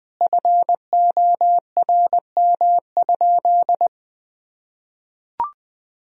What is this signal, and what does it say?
Morse code, heard as a keyed tone, about 20 wpm, 700 Hz. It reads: FORM?
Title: Sentences from Top 1000 Words - Review Part 1 - 20wpm